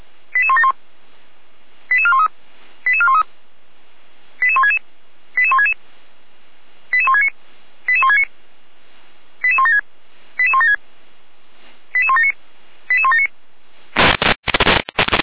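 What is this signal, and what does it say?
Five-tone selective call (ZVEI), heard as quick runs of five tones, tones of about 70 ms. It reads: 88171 88413 88413 88166 88166 88169 88169 88176 88176 88179 88179
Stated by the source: ZVEI five-tone selective call received with an SDR.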